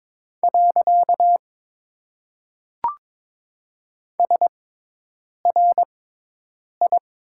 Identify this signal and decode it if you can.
Morse code, heard as a keyed tone, about 22 wpm, 700 Hz. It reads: . S R I
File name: Top 90 QSO Elements - 6x Character Spacing - 22wpm